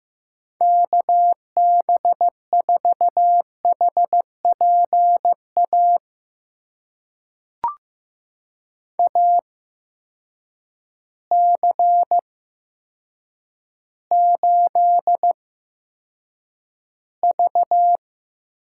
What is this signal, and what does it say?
Morse code, heard as a keyed tone, about 15 wpm, spaced as for 7 wpm, 700 Hz. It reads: KB4HPA AC8V
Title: USA Call Signs - 8x Character Spacing - 15wpm